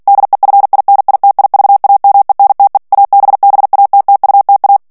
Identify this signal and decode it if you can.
Morse code, heard as a keyed tone, about 48 wpm, 800 Hz. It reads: DEFINITIVAMENTE ABBATTUTA
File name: morse mp3 / ITALIEN texte clair 1 vitesse 45 mots minute